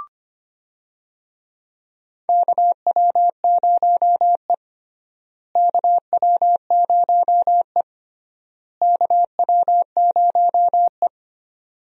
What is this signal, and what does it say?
Morse code, heard as a keyed tone, about 25 wpm, 700 Hz. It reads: KW0E KW0E KW0E
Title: USA Call Signs - 4 Characters - Repeated 3x - 25wpm